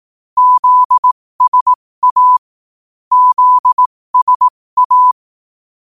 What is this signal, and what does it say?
Morse code, heard as a keyed tone, about 18 wpm, spaced as for 15 wpm, 1000 Hz. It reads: ZSA ZSA
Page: 15 WPM morse code quotes